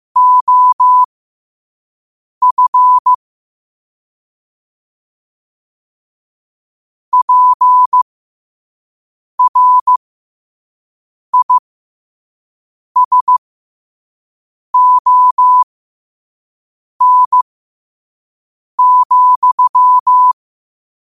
Morse code, heard as a keyed tone, about 16 wpm, spaced as for 5 wpm, 1000 Hz. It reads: OF PRISON,
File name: qotdfiles / 5wpm